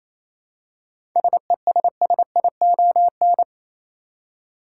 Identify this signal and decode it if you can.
Morse code, heard as a keyed tone, about 27 wpm, 700 Hz. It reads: SESSION
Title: Seven Letter Words - 28wpm